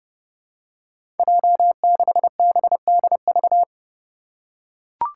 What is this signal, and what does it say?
Morse code, heard as a keyed tone, about 31 wpm, 700 Hz. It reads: J6BDV
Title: Five Challenging Letter-Number Combinations 30wpm